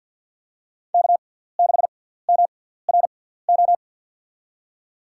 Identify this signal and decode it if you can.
Morse code, heard as a keyed tone, about 53 wpm, spaced as for 17 wpm, 700 Hz. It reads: K6MRO
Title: USA Call Signs - 6x Character Spacing - 50wpm